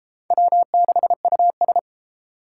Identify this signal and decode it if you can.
Morse code, heard as a keyed tone, about 32 wpm, 690 Hz. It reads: W6US